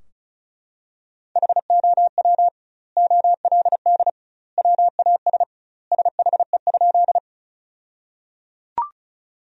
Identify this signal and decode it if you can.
Morse code, heard as a keyed tone, about 34 wpm, 700 Hz. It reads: HOW OLD WAS SHE?
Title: 2x Word-Spacing - Sentences from Top 200 Words - 35wpm